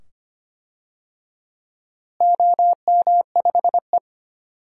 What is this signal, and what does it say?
Morse code, heard as a keyed tone, about 25 wpm, 700 Hz. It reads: OM5E
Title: DXCC-100 Call Signs - 4 Characters - Repeated 2x - 25wpm